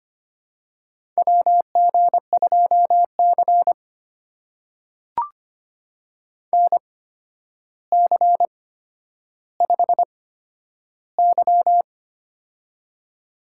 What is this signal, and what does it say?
Morse code, heard as a keyed tone, about 25 wpm, spaced as for 11 wpm, 700 Hz. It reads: WG2C NC5Y
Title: USA Call Signs - 4 Characters - 8x Character Spacing - 25wpm